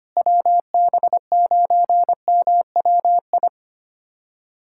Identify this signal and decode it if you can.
Morse code, heard as a keyed tone, about 25 wpm, 700 Hz. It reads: WB9MWI
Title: USA Callsign DE Callsign 25wpm